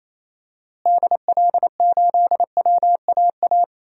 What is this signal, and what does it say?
Morse code, heard as a keyed tone, about 28 wpm, 700 Hz. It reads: DL8WAA
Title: DXCC-100 Call Signs - 6 Characters - Repeated 2x - 28wpm